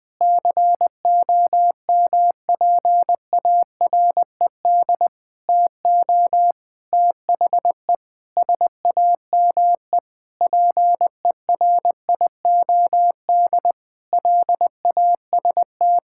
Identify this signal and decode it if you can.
Morse code, heard as a keyed tone, about 20 wpm, 700 Hz. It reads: COMPARED TO THE SAME PERIOD LAST
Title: Sentences with 7-Word Phrases - 20wpm